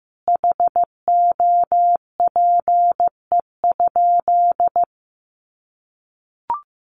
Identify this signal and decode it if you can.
Morse code, heard as a keyed tone, about 15 wpm, 700 Hz. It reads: HOPE?
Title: Sentences from Top 1300 Words 15wpm